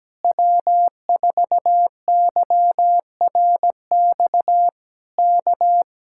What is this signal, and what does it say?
Morse code, heard as a keyed tone, about 17 wpm, 700 Hz. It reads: W4YRX K
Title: Speed-Racing Common QSO Phrases Encore - 17wpm